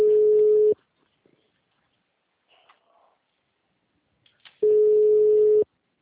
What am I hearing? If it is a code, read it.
Ringback tone (Europe, 425 Hz).
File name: out- -2006- - - .40428